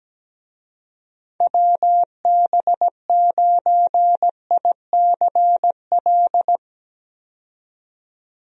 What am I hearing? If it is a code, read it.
Morse code, heard as a keyed tone, about 17 wpm, 700 Hz. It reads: WB9ICL
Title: USA Call Signs - 6 Characters - 17wpm